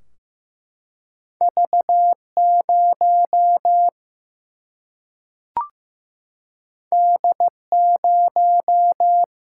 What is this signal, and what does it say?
Morse code, heard as a keyed tone, about 15 wpm, 700 Hz. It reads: V0 D0
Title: Single-Letter Single-Number 15wpm